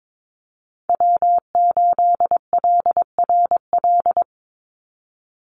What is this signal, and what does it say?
Morse code, heard as a keyed tone, about 22 wpm, 700 Hz. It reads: W8LRL
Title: USA Call Signs 22wpm